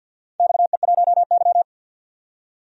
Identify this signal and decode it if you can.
Morse code, heard as a keyed tone, about 52 wpm, 700 Hz. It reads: XE1Y
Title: DXCC-100 Call Signs - 4 Characters - 50wpm